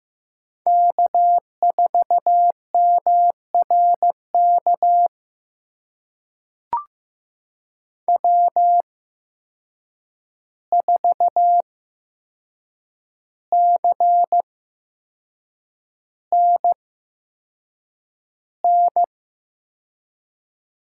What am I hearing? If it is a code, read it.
Morse code, heard as a keyed tone, about 15 wpm, spaced as for 6 wpm, 700 Hz. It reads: K4MRK W4CNN
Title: USA Call Signs - 5 Characters - 8x Character Spacing - 15wpm